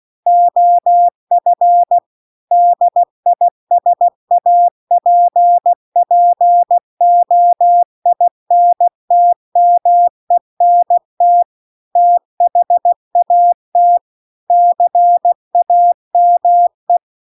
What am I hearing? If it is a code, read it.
Morse code, heard as a keyed tone, about 16 wpm, 700 Hz. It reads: OF DISAPPOINTMENT THAT CAME
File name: CW-The-Big-Twoheaded-River-2-16WPM